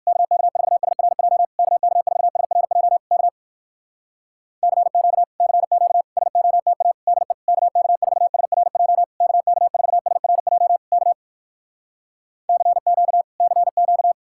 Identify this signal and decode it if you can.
Morse code, heard as a keyed tone, about 58 wpm, 700 Hz. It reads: KK4SRJ KK4SRJ K CQ CQ SOTA DE KK4SRJ KK4SRJ K CQ CQ